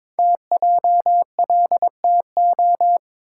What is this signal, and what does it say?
Morse code, heard as a keyed tone, about 22 wpm, 700 Hz. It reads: TJLTO
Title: Code Groups - 5 Letters - 22wpm